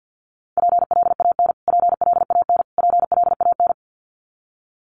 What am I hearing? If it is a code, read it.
Morse code, heard as a keyed tone, about 50 wpm, 700 Hz. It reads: PLAN PLAN PLAN
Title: Top 500 Words Repeated 50wpm